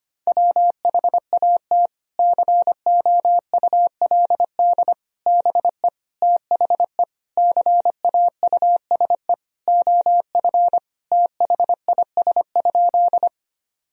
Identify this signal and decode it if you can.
Morse code, heard as a keyed tone, about 25 wpm, 700 Hz. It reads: WHAT COULD BE THE CAUSE OF THIS?
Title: Sentences from Top 1000 Words - Review Part 1 - 25wpm